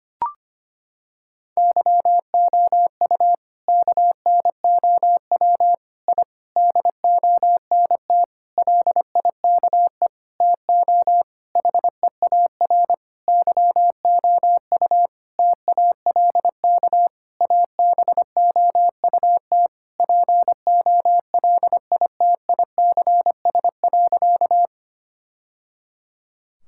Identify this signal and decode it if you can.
Morse code, heard as a keyed tone, about 25 wpm, 700 Hz. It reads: YOU KNOW I DONT LIKE TO HEAR YOU TALK ABOUT POLITICS.
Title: Sentences from Top 1000 Words - Review Part 6 - 25wpm